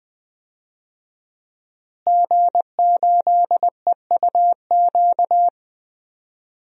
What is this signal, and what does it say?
Morse code, heard as a keyed tone, about 20 wpm, 700 Hz. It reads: G8EUQ
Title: DXCC-100 Call Signs - 5 Characters - Repeated 3x - 20wpm